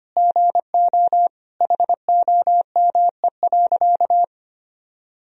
Morse code, heard as a keyed tone, about 25 wpm, 700 Hz. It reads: GO HOME.
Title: Sentences from Top 300 Words and Abbreviations 25wpm